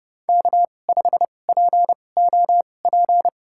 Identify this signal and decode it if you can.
Morse code, heard as a keyed tone, about 30 wpm, spaced as for 22 wpm, 700 Hz. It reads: K5POP
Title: USA Call Signs - 5 Characters - 2x Character Spacing - 30wpm